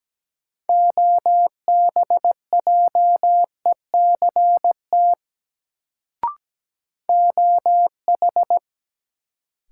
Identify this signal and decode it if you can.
Morse code, heard as a keyed tone, about 17 wpm, 700 Hz. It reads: OBJECT OH
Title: Sentences from Top 500 Words - New Word Drill - 17wpm